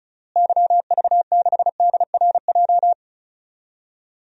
Morse code, heard as a keyed tone, about 36 wpm, 700 Hz. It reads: YV6DRJ
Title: DXCC-100 Call Signs - 6 Characters - Repeated 2x - 35wpm